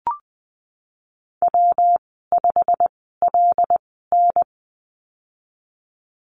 Morse code, heard as a keyed tone, about 19 wpm, spaced as for 14 wpm, 700 Hz. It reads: W5LN
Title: 207 Lesson 33g - Letter j - Call Signs 20wpm